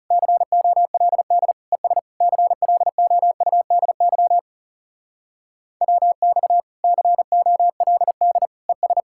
Morse code, heard as a keyed tone, about 40 wpm, 700 Hz. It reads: COLD ES CLOUDY WX COLD ES